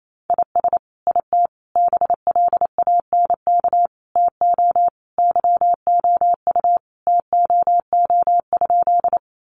Morse code, heard as a keyed tone, about 28 wpm, 700 Hz. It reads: IS IT BLANK TO YOU TOO?
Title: Sentences from Top 1500 Words 28wpm